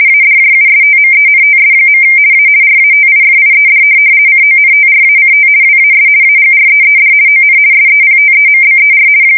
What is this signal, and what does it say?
RTTY, about 50 baud. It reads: ADJUSTMENTS. A RATIFIED CONTRACT WAS AGREED ON SEPTEMBER
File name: fsk170-50B / teletype-corp